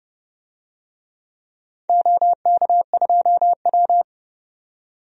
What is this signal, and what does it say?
Morse code, heard as a keyed tone, about 30 wpm, 700 Hz. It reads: OK2W
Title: DXCC-100 Call Signs - 4 Characters - 30wpm